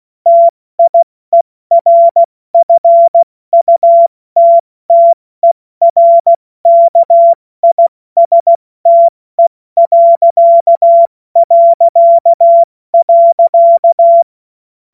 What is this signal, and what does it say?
Morse code, heard as a keyed tone, about 16 wpm, 680 Hz. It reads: TIERFUTTERKISTE...
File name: Witz-16-September-2025-70-F80